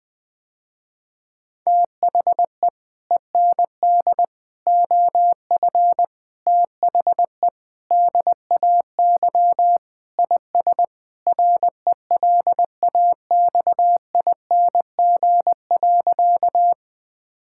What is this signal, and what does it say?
Morse code, heard as a keyed tone, about 20 wpm, 700 Hz. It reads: THE END OF THE DAY IS RELAXING.
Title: Sentences with 3-Word Phrases - Spoken Beginning - 20wpm